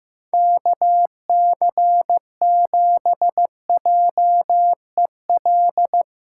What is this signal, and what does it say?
Morse code, heard as a keyed tone, about 15 wpm, 700 Hz. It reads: KC7JEL
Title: USA Call Signs - 8x Character Spacing - 15wpm